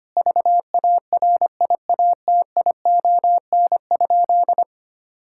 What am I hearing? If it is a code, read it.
Morse code, heard as a keyed tone, about 25 wpm, 700 Hz. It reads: VARIATION?
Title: Sentences from Top 1600 Words 25wpm